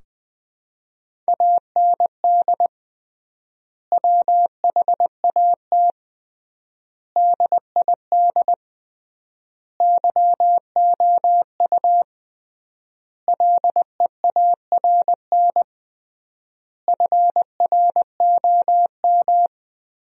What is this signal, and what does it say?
Morse code, heard as a keyed tone, about 21 wpm, 700 Hz. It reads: AND WHAT DID YOU LEARN FROM
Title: 3x Word-Spacing - Sentences from Top 300 Words - 20wpm